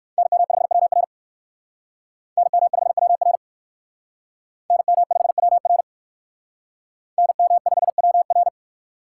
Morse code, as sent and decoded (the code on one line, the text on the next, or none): -. -- ..... .-- .-. / -. -- ..... .-- .-. / -. -- ..... .-- .-. / -. -- ..... .-- .-.
NM5WR NM5WR NM5WR NM5WR